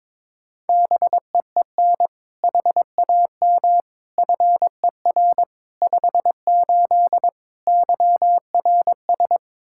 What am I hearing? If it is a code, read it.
Morse code, heard as a keyed tone, about 22 wpm, 700 Hz. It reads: BEEN HAM FER 58 YRS
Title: Practice QSO - Exchange 3 - More Info - 22wpm